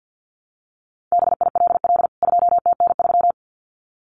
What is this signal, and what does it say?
Morse code, heard as a keyed tone, about 48 wpm, 700 Hz. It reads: BILL 2TN3
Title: CWT Exchange 50wpm